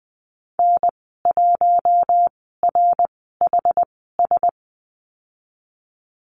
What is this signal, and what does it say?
Morse code, heard as a keyed tone, about 20 wpm, spaced as for 14 wpm, 700 Hz. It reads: N1RHS
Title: 076 Lesson 13g - Letter L - Call Signs 20wpm